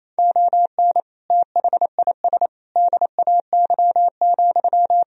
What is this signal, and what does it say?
Morse code, heard as a keyed tone, about 28 wpm, 700 Hz. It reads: ON THIS DAY,
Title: Sentences from Top 1000 Words - Review Part 2 - 28wpm